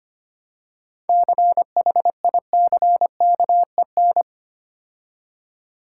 Morse code, heard as a keyed tone, about 25 wpm, 700 Hz. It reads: CHICKEN
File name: practice / 25wpm / Difficult Words with Hints - 25wpm